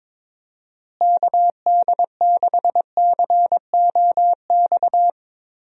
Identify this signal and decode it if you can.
Morse code, heard as a keyed tone, about 22 wpm, 700 Hz. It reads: KD6COX
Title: USA Call Signs - 6 Characters - Repeated 2x - 22wpm